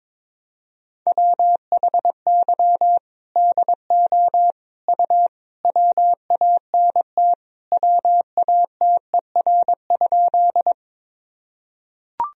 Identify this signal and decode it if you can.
Morse code, heard as a keyed tone, about 22 wpm, 700 Hz. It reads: WHY DO U WANT WATER?
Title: Sentences from Top 200 Words and Abbreviations 22wpm